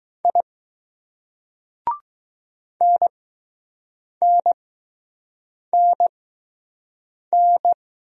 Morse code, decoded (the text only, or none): I NNNN